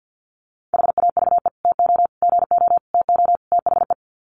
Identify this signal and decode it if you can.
Morse code, heard as a keyed tone, about 50 wpm, 700 Hz. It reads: HAVE TO GO TO THE
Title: Sets of 5 Words Encore 50wpm